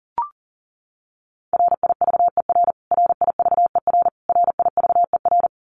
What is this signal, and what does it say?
Morse code, heard as a keyed tone, about 40 wpm, 700 Hz. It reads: RIVER RIVER RIVER
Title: Top 500 Words Repeated 40wpm